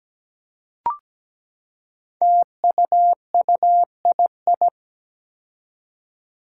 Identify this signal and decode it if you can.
Morse code, heard as a keyed tone, about 17 wpm, 700 Hz. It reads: TUUII
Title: Five Challenging Letter-Number Combinations 17wpm